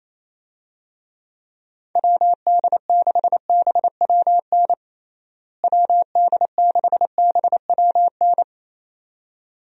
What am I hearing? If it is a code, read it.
Morse code, heard as a keyed tone, about 28 wpm, 700 Hz. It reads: WD6BWN WD6BWN
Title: USA Call Signs - 6 Characters - Repeated 2x - 28wpm